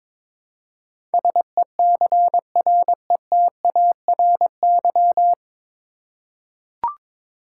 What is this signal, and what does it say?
Morse code, heard as a keyed tone, about 22 wpm, 700 Hz. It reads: SECRETARY
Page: Morse Code Ninja practice 22wpm